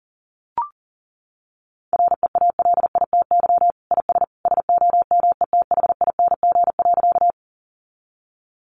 Morse code, heard as a keyed tone, about 40 wpm, 700 Hz. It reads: REALITY IS SOMETHING.
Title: Sentences from Top 1200 Words 40wpm